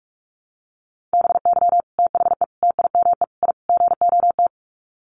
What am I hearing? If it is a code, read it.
Morse code, heard as a keyed tone, about 44 wpm, 700 Hz. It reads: BY THE TIME I GOT
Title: Sets of 5 Words Encore2 45wpm